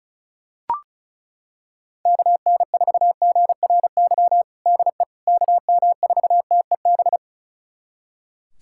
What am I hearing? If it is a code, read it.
Morse code, heard as a keyed tone, about 36 wpm, 700 Hz. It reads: KN4GRY DE KM4TEB
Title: USA Callsign DE Callsign 35wpm